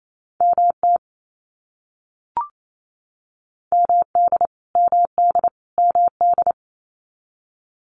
Morse code, heard as a keyed tone, about 28 wpm, 700 Hz. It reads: MT MD MD MD
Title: US States Repeated 3x 28wpm